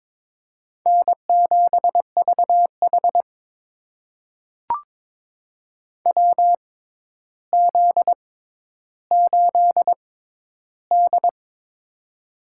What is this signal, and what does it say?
Morse code, heard as a keyed tone, about 22 wpm, spaced as for 11 wpm, 700 Hz. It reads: N7VH WZ8D
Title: USA Call Signs - 4 Characters - 6x Character Spacing - 22wpm